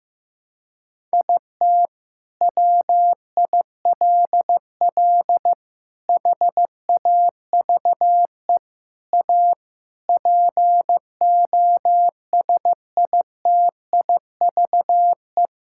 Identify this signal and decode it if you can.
Morse code, heard as a keyed tone, about 15 wpm, 700 Hz. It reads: IT WILL HAVE A POSITIVE
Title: Sentences from Top 1000 Words - Review Part 6 - 15wpm